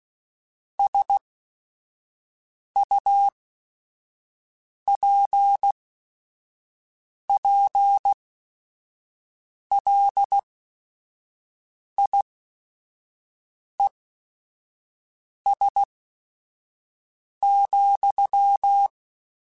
Morse code, heard as a keyed tone, about 16 wpm, spaced as for 5 wpm, 800 Hz. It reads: SUPPLIES,
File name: Technical Subjects / Technical Subjects-mp3s / Sw Pwr Sup @5